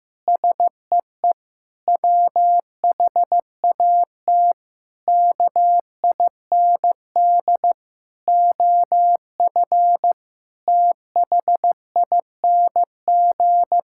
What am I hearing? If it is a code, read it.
Morse code, heard as a keyed tone, about 15 wpm, 700 Hz. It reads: SEE WHAT KIND OF THING
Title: Sentences from Top 1000 Words - Review Part 6 - 15wpm